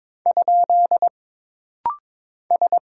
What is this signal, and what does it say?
Morse code, heard as a keyed tone, about 22 wpm, 700 Hz. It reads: ? S
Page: This rapid-fire timing is geared to push your ICR (Instant Character Recognition) skills.